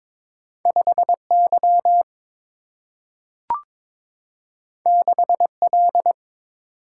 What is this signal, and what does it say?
Morse code, heard as a keyed tone, about 22 wpm, 700 Hz. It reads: 5Y 6L
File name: practice / 22wpm / Two Letter-Number Combinations - 22wpm